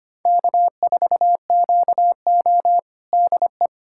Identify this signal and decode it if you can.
Morse code, heard as a keyed tone, about 25 wpm, 700 Hz. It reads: K4QO DE
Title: USA Callsign DE Callsign - 4 Characters - 25wpm